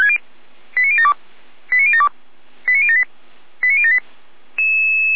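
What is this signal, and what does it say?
Five-tone selective call (ZVEI), heard as quick runs of five tones, tones of about 70 ms. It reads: 89072 89072 89078 89078
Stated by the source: ZVEI five-tone call received on SDR.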